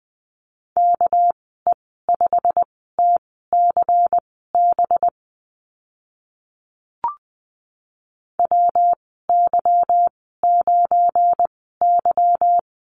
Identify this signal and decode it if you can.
Morse code, heard as a keyed tone, about 20 wpm, spaced as for 14 wpm, 700 Hz. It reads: KE5TCB WY9Y
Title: 207 Lesson 33g - Letter j - Call Signs 20wpm